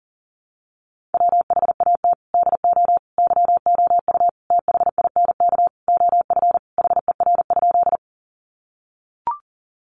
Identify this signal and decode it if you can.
Morse code, heard as a keyed tone, about 37 wpm, 700 Hz. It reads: WHAT DO YOU THINK OF HER?
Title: Sentences from Top 200 Words 40wpm